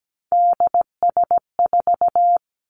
Morse code, heard as a keyed tone, about 17 wpm, 700 Hz. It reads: DS4
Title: Three Challenging Letter-Number Combinations 17wpm